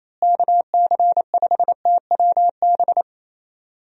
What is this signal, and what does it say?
Morse code, heard as a keyed tone, about 27 wpm, 700 Hz. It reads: KC5TWB
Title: USA Call Signs - 6 Characters - Repeated 2x - 28wpm